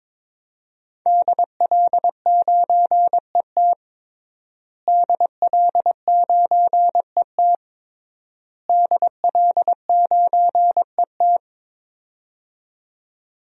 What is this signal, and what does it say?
Morse code, heard as a keyed tone, about 22 wpm, 700 Hz. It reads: DL9ET DL9ET DL9ET
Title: DXCC-100 Call Signs - 5 Characters - Repeated 3x - 22wpm